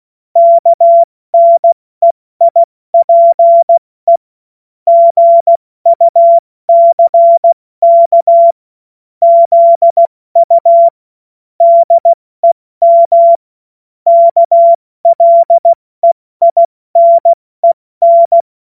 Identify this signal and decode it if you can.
Morse code, heard as a keyed tone, about 16 wpm, 680 Hz. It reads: KNEIPE GUCK ZU DEM KLEINEN
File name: Morsekurs / Witze / Witz 22. Februar 2022 70 F80